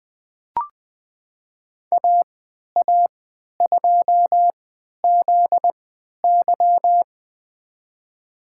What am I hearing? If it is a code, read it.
Morse code, heard as a keyed tone, about 20 wpm, spaced as for 11 wpm, 700 Hz. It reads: AA2ZY
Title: USA Call Signs - 3x Character Spacing - 20wpm